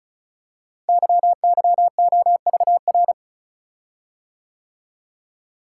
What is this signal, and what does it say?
Morse code, heard as a keyed tone, about 36 wpm, 700 Hz. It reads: YYOVR
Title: Code Groups - 5 Letters - 35wpm